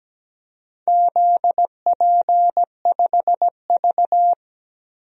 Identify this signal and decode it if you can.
Morse code, heard as a keyed tone, about 17 wpm, 700 Hz. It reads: ZP5V
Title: DXCC-100 Call Signs - 4 Characters - Repeated 3x - 17wpm